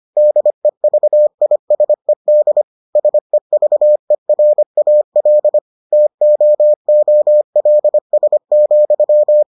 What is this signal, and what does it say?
Morse code, heard as a keyed tone, about 25 wpm, 600 Hz. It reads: DEVISED SEVERAL TOOLS,